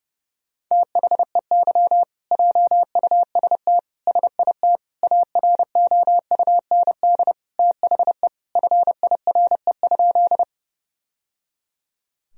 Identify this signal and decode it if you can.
Morse code, heard as a keyed tone, about 30 wpm, 700 Hz. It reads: THEY JUST SIT AROUND THE FIRE?
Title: Sentences from Top 1000 Words - Review Part 3 - 30wpm